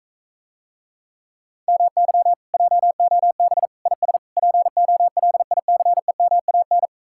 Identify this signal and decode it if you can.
Morse code, heard as a keyed tone, about 45 wpm, 700 Hz. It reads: MY JOB IS POLICEMAN